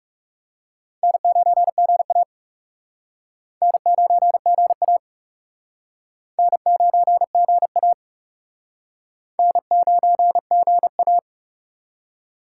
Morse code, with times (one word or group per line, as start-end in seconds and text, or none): 1.0-2.2 N9GA
3.6-5.0 N9GA
6.4-7.9 N9GA
9.4-11.2 N9GA